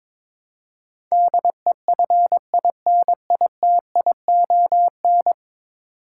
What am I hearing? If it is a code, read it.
Morse code, heard as a keyed tone, about 22 wpm, 700 Hz. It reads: DEFINITION
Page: morse-code-ninja-practice-22wpm